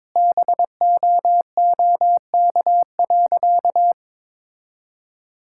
Morse code, heard as a keyed tone, about 21 wpm, 700 Hz. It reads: BOOK.
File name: practice / 22wpm / Sentences with 4-Word Phrases - 22wpm